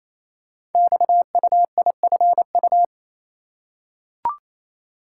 Morse code, heard as a keyed tone, about 28 wpm, 700 Hz. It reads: XUIFU
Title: Code Groups - 5 Letters - 28wpm